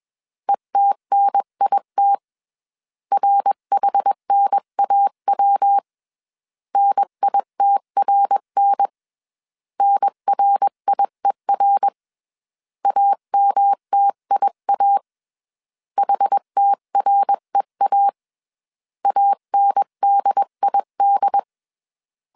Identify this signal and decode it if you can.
Morse code, heard as a keyed tone, about 21 wpm, 800 Hz. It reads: ETNIT RHNAW NITRN NRIER AMTIA HTREA ANDID
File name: 20wpm